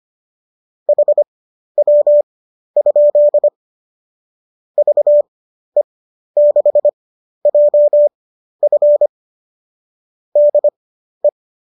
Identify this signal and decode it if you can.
Morse code, heard as a keyed tone, about 25 wpm, spaced as for 12 wpm, 590 Hz. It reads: HW? VE6JF DE